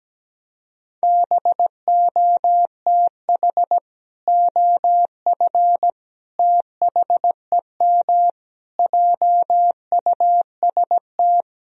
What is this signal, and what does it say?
Morse code, heard as a keyed tone, about 17 wpm, 700 Hz. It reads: BOTH OF THEM JUST
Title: Sentences from Top 1000 Words - Review Part 2 - 17wpm